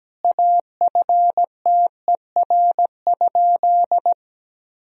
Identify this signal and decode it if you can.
Morse code, heard as a keyed tone, about 17 wpm, 700 Hz. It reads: AFTER?
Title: Sentences from Top 1000 Words - Review Part 1 - 17wpm